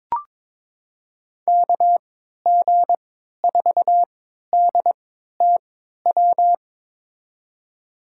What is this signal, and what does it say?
Morse code, heard as a keyed tone, about 22 wpm, spaced as for 12 wpm, 700 Hz. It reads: KG4DTW
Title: USA Call Signs - 3x Character Spacing - 22wpm